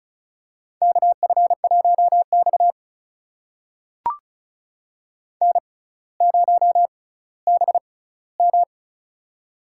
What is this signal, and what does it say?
Morse code, heard as a keyed tone, about 36 wpm, spaced as for 18 wpm, 700 Hz. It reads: KF1X N0BM
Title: USA Call Signs - 4 Characters - 6x Character Spacing - 35wpm